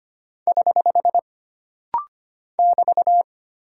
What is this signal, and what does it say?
Morse code, heard as a keyed tone, about 25 wpm, 700 Hz. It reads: <HH> <BT>
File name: Single Punctuation-Prosign - Rapid-Fire - 25wpm